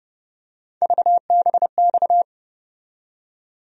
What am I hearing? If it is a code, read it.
Morse code, heard as a keyed tone, about 31 wpm, 700 Hz. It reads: VBX
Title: USA Call Sign Postfixes - 3 Letters 30wpm